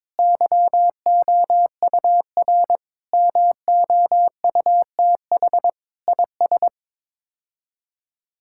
Morse code, heard as a keyed tone, about 22 wpm, 700 Hz. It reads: YOUR MOUTH IS
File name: practice / 22wpm / Sets of 7 Words - 22wpm